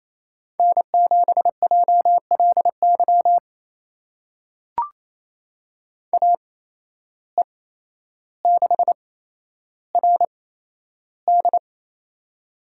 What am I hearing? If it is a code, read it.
Morse code, heard as a keyed tone, about 27 wpm, spaced as for 12 wpm, 700 Hz. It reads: N7JLY AE6RD
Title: USA Call Signs - 8x Character Spacing - 28wpm